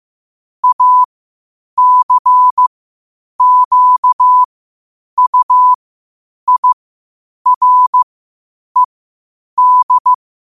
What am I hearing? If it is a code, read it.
Morse code, heard as a keyed tone, about 15 wpm, spaced as for 8 wpm, 1000 Hz. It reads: ACQUIRED
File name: qotdfiles / 8wpm